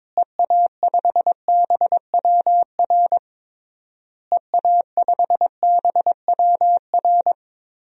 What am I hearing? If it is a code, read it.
Morse code, heard as a keyed tone, about 21 wpm, 700 Hz. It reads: EA5BWR EA5BWR
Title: DXCC-100 Call Signs - 6 Characters - Repeated 3x - 22wpm